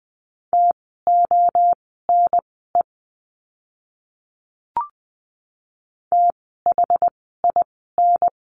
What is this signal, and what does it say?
Morse code, heard as a keyed tone, about 20 wpm, spaced as for 14 wpm, 700 Hz. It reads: TONE THIN
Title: 068 Lesson 12f - Letter d - Words 20wpm